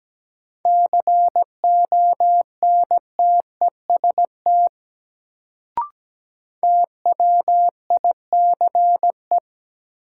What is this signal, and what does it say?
Morse code, heard as a keyed tone, about 17 wpm, 700 Hz. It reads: CONTEST TWICE
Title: Sentences from Top 1600 Words - New Word Drill - 17wpm